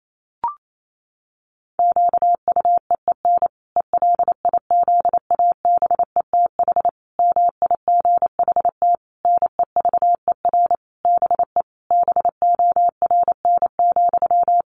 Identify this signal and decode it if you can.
Morse code, heard as a keyed tone, about 28 wpm, 700 Hz. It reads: QUEEN ELIZABETH MIGHT NEVER BE BORN,
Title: A Sound Of Thunder 28wpm